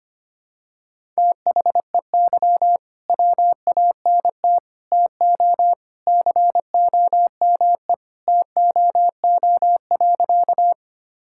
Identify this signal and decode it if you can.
Morse code, heard as a keyed tone, about 25 wpm, 700 Hz. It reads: THEY WANT TO COME TOO.
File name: practice / 25wpm / Sentences with 3-Word Phrases - 25wpm